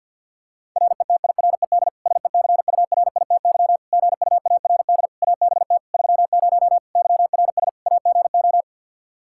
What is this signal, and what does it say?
Morse code, heard as a keyed tone, about 50 wpm, 700 Hz. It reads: RETIRED SECURITY GUARD ABT 30 YRS AGO
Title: K1USN SST QSO Practice 50wpm